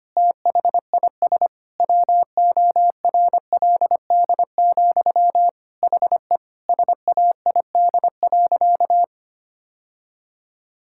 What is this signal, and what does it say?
Morse code, heard as a keyed tone, about 25 wpm, 700 Hz. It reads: THIS WORLD, HE SAID.
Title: Sentences from Top 1600 Words 25wpm